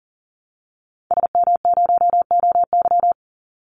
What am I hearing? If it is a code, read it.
Morse code, heard as a keyed tone, about 38 wpm, 700 Hz. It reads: SM0OY
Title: Call Signs - 40wpm